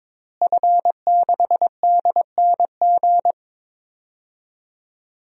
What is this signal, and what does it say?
Morse code, heard as a keyed tone, about 22 wpm, 700 Hz. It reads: F6DNG
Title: DXCC-100 Call Signs - 5 Characters - Repeated 3x - 22wpm